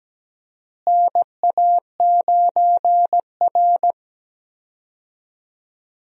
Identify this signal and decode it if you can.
Morse code, heard as a keyed tone, about 17 wpm, 700 Hz. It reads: NA9R
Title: USA Call Signs - 4 Characters - Repeated 2x - 17wpm